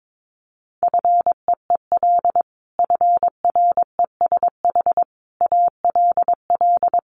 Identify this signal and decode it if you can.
Morse code, heard as a keyed tone, about 22 wpm, 700 Hz. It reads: FEEL FRESH ALL
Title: Sentences from Top 1200 Words 22wpm